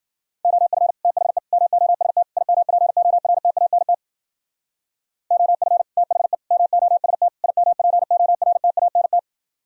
Morse code, heard as a keyed tone, about 47 wpm, 700 Hz. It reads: OF THE MOST IMPORTANT OF THE MOST IMPORTANT
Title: Speed-Racing - Sets of 4 Words 40wpm